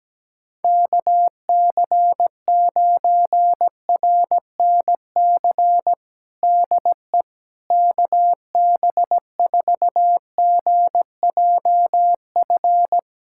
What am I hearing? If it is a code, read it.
Morse code, heard as a keyed tone, about 17 wpm, 700 Hz. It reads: KC9RNC DE KB4GJF
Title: USA Callsign DE Callsign 17wpm